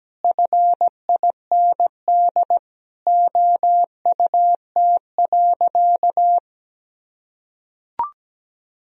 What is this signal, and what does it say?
Morse code, heard as a keyed tone, about 17 wpm, 700 Hz. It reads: FIND OUT.
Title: Sentences from Top 200 Words and Abbreviations 17wpm